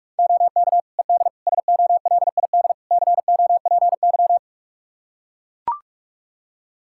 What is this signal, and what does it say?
Morse code, heard as a keyed tone, about 41 wpm, 700 Hz. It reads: OK ED SOLID COPY
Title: Practice QSO - Exchange 3 - More Info - 45wpm